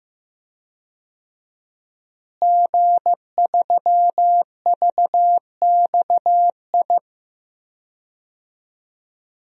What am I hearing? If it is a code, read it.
Morse code, heard as a keyed tone, about 16 wpm, 700 Hz. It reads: G3VXI
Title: DXCC-100 Call Signs - 5 Characters - Repeated 2x - 15wpm